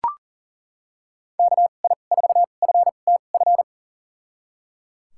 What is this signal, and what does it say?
Morse code, heard as a keyed tone, about 37 wpm, spaced as for 28 wpm, 700 Hz. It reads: KI4FTF
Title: USA Call Signs - 6 Characters - 2x Character Spacing - 40wpm